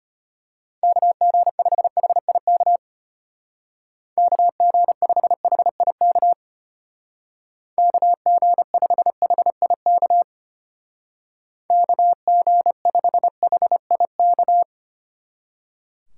Morse code, heard as a keyed tone, about 32 wpm, 700 Hz. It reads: KG5HIK KG5HIK KG5HIK KG5HIK